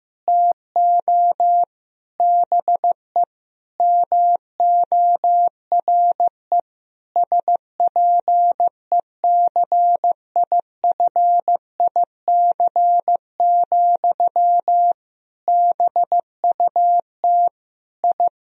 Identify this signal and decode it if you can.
Morse code, heard as a keyed tone, about 16 wpm, 700 Hz. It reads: TO BE MORE SPECIFIC, BUT I
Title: Sentences from Top 1000 Words - Review Part 3 - 15wpm